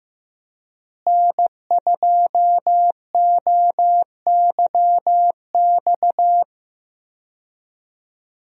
Morse code, heard as a keyed tone, about 15 wpm, 700 Hz. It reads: N2OYX
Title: USA Call Signs - 5 Characters - Repeated 2x - 15wpm